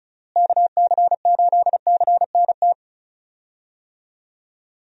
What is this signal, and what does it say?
Morse code, heard as a keyed tone, about 36 wpm, 700 Hz. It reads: KC8CNT
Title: USA Call Signs - 6 Characters - Repeated 2x - 35wpm